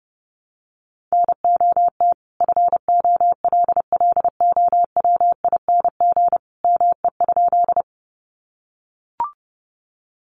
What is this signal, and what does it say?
Morse code, heard as a keyed tone, about 28 wpm, 700 Hz. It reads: NOT FOLLOWING ME?
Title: Sentences from Top 700 Words 30wpm